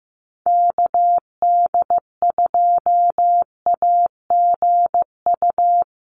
Morse code, heard as a keyed tone, about 15 wpm, 700 Hz. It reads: KD2AGU
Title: USA Call Signs 15wpm